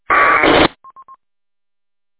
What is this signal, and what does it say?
There is no code.